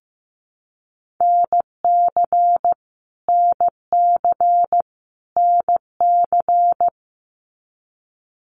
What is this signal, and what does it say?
Morse code, heard as a keyed tone, about 15 wpm, 700 Hz. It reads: NC NC NC
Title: US States Repeated 3x 15wpm